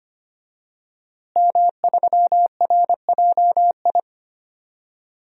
Morse code, heard as a keyed tone, about 25 wpm, 700 Hz. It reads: M3RJI
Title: DXCC-100 Call Signs - 5 Characters - Repeated 2x - 25wpm